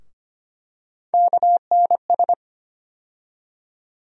Morse code, heard as a keyed tone, about 25 wpm, 700 Hz. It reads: KNS
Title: USA Call Sign Postfixes - 3 Letters 25wpm